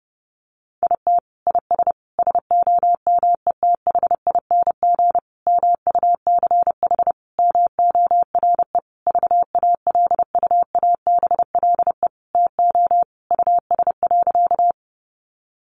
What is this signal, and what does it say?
Morse code, heard as a keyed tone, about 30 wpm, 700 Hz. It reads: IT IS SOMETHING MUCH MORE VALUABLE TO US.
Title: Sentences from Top 1100 Words 30wpm